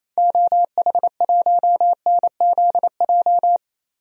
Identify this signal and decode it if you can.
Morse code, heard as a keyed tone, about 28 wpm, 700 Hz. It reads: OH1NZJ
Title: DXCC-100 Call Signs - 6 Characters - 28wpm